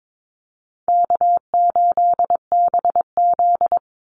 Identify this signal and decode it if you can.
Morse code, heard as a keyed tone, about 22 wpm, 700 Hz. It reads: K8BZ
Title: Easy Call Signs Repeated 3x - 22wpm